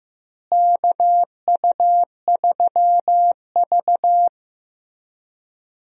Morse code, heard as a keyed tone, about 15 wpm, 700 Hz. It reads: KU3V
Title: USA Call Signs - 4 Characters - Repeated 2x - 15wpm